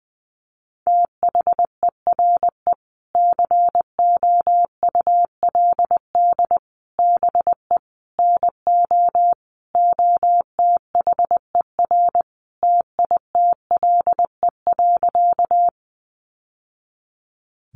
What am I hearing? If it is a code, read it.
Morse code, heard as a keyed tone, about 20 wpm, 700 Hz. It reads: THERE COULD BE NO OTHER TITLE.
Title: Sentences from Top 1200 Words 20wpm